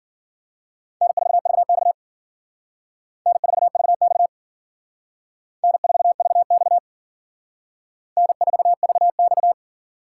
Morse code, heard as a keyed tone, about 53 wpm, 700 Hz. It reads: N4VX N4VX N4VX N4VX